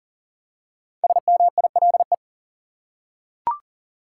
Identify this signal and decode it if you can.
Morse code, heard as a keyed tone, about 40 wpm, 700 Hz. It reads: SMILE
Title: Five Letter Words - 40wpm